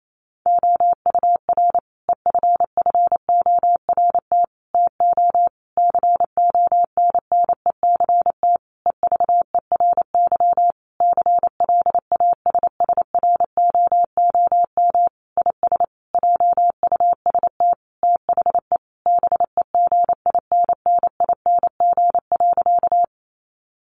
Morse code, heard as a keyed tone, about 28 wpm, 700 Hz. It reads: OUR EFFORT TO CONNECT EVERY CLASSROOM IS JUST THE BEGINNING.
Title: Sentences from Top 1500 Words 28wpm